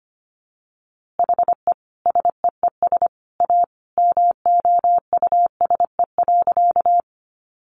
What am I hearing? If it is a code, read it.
Morse code, heard as a keyed tone, about 25 wpm, 700 Hz. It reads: HE SEES A MOUSE.
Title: Easy Phrases 25wpm